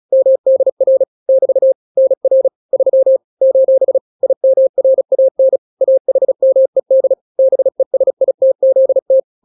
Morse code, heard as a keyed tone, about 36 wpm, 530 Hz. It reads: MDR <BT> NR 3 : IMRAN AHMED BESITZT
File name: daily / 35wpm